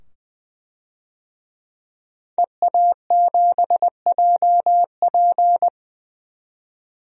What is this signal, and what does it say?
Morse code, heard as a keyed tone, about 20 wpm, 700 Hz. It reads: EA7JP
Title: DXCC-100 Call Signs - 5 Characters - 20wpm